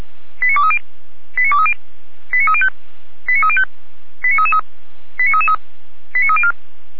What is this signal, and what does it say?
Five-tone selective call (ZVEI), heard as quick runs of five tones, tones of about 70 ms. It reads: 89230 89230 89305 89305 89333 89333 89304
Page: SDR/ZVEI